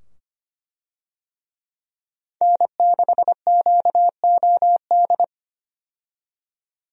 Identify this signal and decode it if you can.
Morse code, heard as a keyed tone, about 25 wpm, 700 Hz. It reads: N6QOD
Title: USA Call Signs - 5 Characters - 25wpm